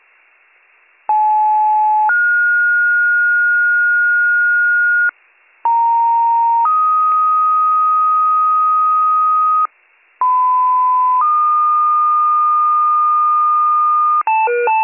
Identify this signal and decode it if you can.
Two-tone page, heard as a one-second tone, then a three-second tone, about 860 then 1430 Hz, about 930 then 1250 Hz, about 1010 then 1220 Hz. It is heard from 1.1 to 14.2 s.